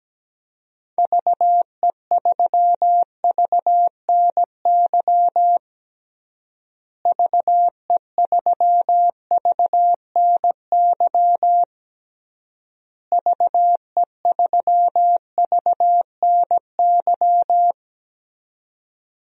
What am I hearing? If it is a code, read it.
Morse code, heard as a keyed tone, about 17 wpm, 700 Hz. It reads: VE3VNY VE3VNY VE3VNY
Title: DXCC-100 Call Signs - 6 Characters - Repeated 3x - 17wpm